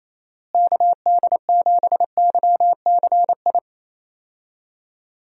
Morse code, heard as a keyed tone, about 28 wpm, 700 Hz. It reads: KD7YCI
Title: USA Call Signs - 6 Characters - Repeated 2x - 28wpm